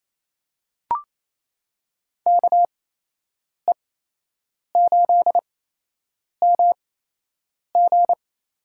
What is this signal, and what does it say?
Morse code, heard as a keyed tone, about 28 wpm, spaced as for 8 wpm, 700 Hz. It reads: KE8MG
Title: USA Call Signs - 8x Character Spacing - 28wpm